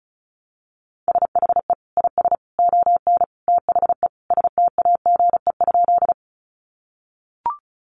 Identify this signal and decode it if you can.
Morse code, heard as a keyed tone, about 34 wpm, 700 Hz. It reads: SHE IS ON THE STAGE?
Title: Sentences from Top 1200 Words 35wpm